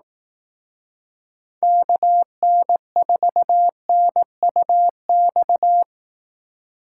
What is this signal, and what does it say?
Morse code, heard as a keyed tone, about 18 wpm, 700 Hz. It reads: KN4NUX
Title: Speed Racing - USA 2x3 Call Signs 15wpm